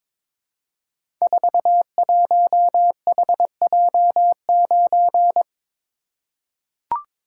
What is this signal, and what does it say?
Morse code, heard as a keyed tone, about 22 wpm, 700 Hz. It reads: 41HJ9
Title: Code Groups - 5 Letter:Numbers - 22wpm